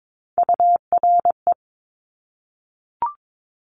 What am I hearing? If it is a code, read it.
Morse code, heard as a keyed tone, about 22 wpm, 700 Hz. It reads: URE
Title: Top Three Letter Combinations in Words 22wpm